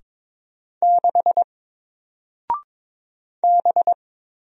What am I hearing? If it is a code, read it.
Morse code, heard as a keyed tone, about 22 wpm, 700 Hz. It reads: 6 B
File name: Single-Letter-Number-22wpm